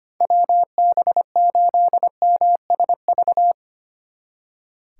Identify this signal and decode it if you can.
Morse code, heard as a keyed tone, about 33 wpm, 700 Hz. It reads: WB8MSV